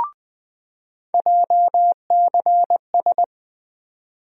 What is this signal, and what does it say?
Morse code, heard as a keyed tone, about 20 wpm, 700 Hz. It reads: JCS
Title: USA Call Sign Postfixes - 3 Letters 20wpm